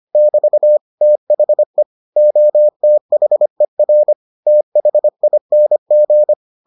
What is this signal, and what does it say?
Morse code, heard as a keyed tone, about 25 wpm, 600 Hz. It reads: <BT> THE OTHER THING